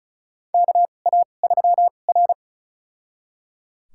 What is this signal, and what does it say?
Morse code, heard as a keyed tone, about 36 wpm, spaced as for 25 wpm, 700 Hz. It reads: KA3R
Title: USA Call Signs - 4 Characters - 2x Character Spacing - 35wpm